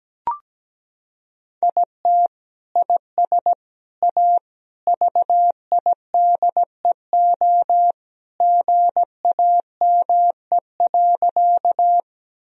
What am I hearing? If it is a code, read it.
Morse code, heard as a keyed tone, about 17 wpm, 700 Hz. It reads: IT IS A VIDEO GAME.
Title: Sentences from Top 1000 Words - Review Part 4 - 17wpm